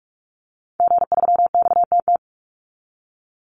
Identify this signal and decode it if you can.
Morse code, heard as a keyed tone, about 44 wpm, 700 Hz. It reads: G3XTT
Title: Call Signs Repeated 2x - 45wpm